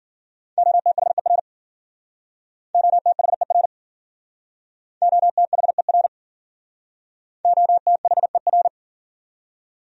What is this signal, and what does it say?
Morse code, heard as a keyed tone, about 51 wpm, 700 Hz. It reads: OTHER OTHER OTHER OTHER